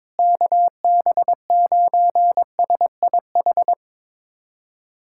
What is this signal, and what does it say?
Morse code, heard as a keyed tone, about 22 wpm, 700 Hz. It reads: KB9SIH
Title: USA Callsign DE Callsign 22wpm